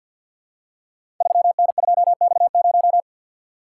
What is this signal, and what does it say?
Morse code, heard as a keyed tone, about 48 wpm, 700 Hz. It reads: 3N2X0
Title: Code Groups - 5 Letter:Numbers - 50wpm